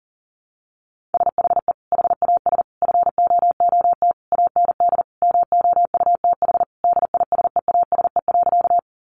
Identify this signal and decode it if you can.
Morse code, heard as a keyed tone, about 40 wpm, 700 Hz. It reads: SHE HAS FOOT AND MOUTH DISEASE.
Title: Sentences from Top 1500 Words 40wpm